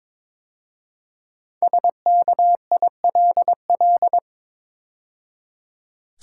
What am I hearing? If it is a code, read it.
Morse code, heard as a keyed tone, about 22 wpm, 700 Hz. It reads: SKILL
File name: practice / 22wpm / 5 Letter Words - First 3 Letters Given - 22wpm